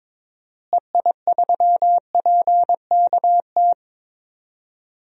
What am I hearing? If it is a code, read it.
Morse code, heard as a keyed tone, about 21 wpm, 700 Hz. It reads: EI3PKT
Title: DXCC-100 Call Signs - 6 Characters - Repeated 2x - 22wpm